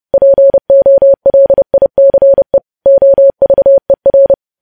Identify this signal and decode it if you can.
Morse code, heard as a keyed tone, about 30 wpm, 550 Hz. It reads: POLICE OVER